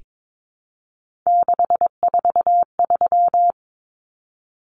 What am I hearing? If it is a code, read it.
Morse code, heard as a keyed tone, about 22 wpm, 700 Hz. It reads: 643
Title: Numbers - 3-Digit 22wpm